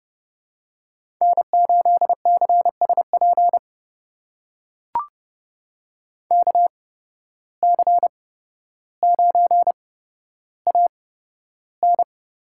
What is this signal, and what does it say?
Morse code, heard as a keyed tone, about 30 wpm, spaced as for 13 wpm, 700 Hz. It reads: N8CSP KC9AN
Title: USA Call Signs - 8x Character Spacing - 30wpm